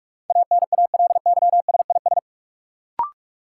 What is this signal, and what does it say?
Morse code, heard as a keyed tone, about 46 wpm, 700 Hz. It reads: ANALYSIS
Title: Sentences from Top 1400 Words - New Word Drill Alt - 45wpm